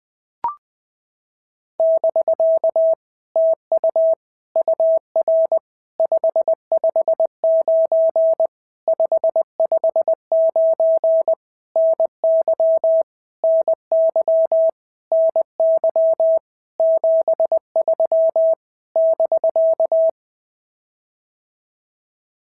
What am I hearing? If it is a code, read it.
Morse code, heard as a keyed tone, about 20 wpm, 650 Hz. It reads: <BK> TU UR 559 559 NY NY NY 73 <BK>
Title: POTA QSO Practice - 20wpm